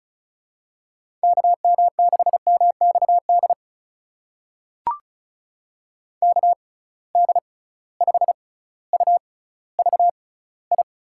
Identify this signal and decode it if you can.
Morse code, heard as a keyed tone, about 36 wpm, spaced as for 18 wpm, 700 Hz. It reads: KM6MXD KD5UVI
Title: USA Call Signs - 6 Characters - 6x Character Spacing - 35wpm